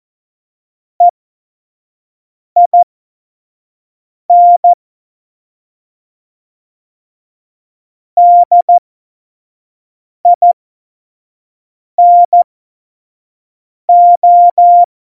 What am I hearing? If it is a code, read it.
Morse code, heard as a keyed tone, about 14 wpm, spaced as for 5 wpm, 700 Hz. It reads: EIN DINO
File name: Morsekurs / Witze / Witz 04. September 2018, 25 F70